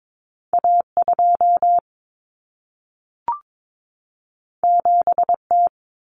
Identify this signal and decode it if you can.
Morse code, heard as a keyed tone, about 22 wpm, 700 Hz. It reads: A2 7T
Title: Common Cut Numbers - 2-Digit 22wpm